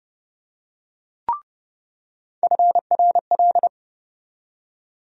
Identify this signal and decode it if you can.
Morse code, heard as a keyed tone, about 30 wpm, 700 Hz. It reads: FRL
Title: USA Call Sign Postfixes - 3 Letters 30wpm